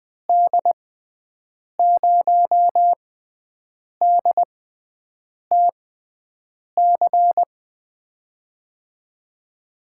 Morse code, heard as a keyed tone, about 20 wpm, spaced as for 7 wpm, 700 Hz. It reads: D0DTC
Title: USA Call Signs - 6 Characters - 6x Character Spacing - 20wpm